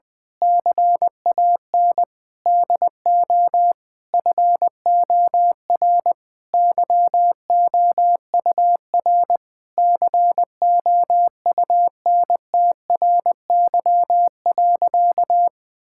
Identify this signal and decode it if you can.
Morse code, heard as a keyed tone, about 20 wpm, 700 Hz. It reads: CAN DO FOR YOUR COUNTRY.
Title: Sentences with 9-Word Phrases - Spoken Beginning - 20wpm